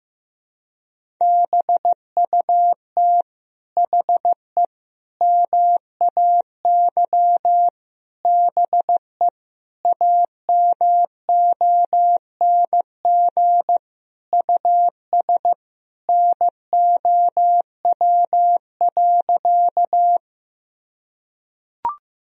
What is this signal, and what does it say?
Morse code, heard as a keyed tone, about 15 wpm, 700 Hz. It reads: BUT HE MAY BE AMONG US NOW.
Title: Sentences from Top 1000 Words - Review Part 3 - 15wpm